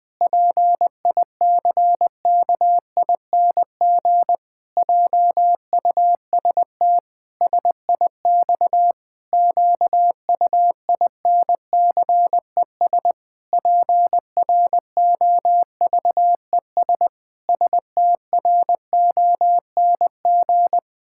Morse code, heard as a keyed tone, about 20 wpm, 700 Hz. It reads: PICKING JUST SIX QUINCES PROVES STRONG
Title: Pangram Challenge v2 - 20wpm